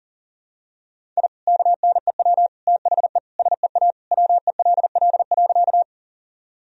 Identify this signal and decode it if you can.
Morse code, heard as a keyed tone, about 40 wpm, 700 Hz. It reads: I KNEW THE SEA WELL.
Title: Sentences from Top 1000 Words - Review Part 1 - 40wpm